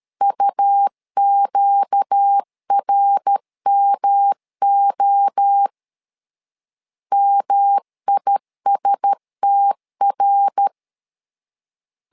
Morse code, heard as a keyed tone, about 13 wpm, 800 Hz. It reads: UQRMO MISTR
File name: LEZIONI / 4 Q1S3 12wpm 2023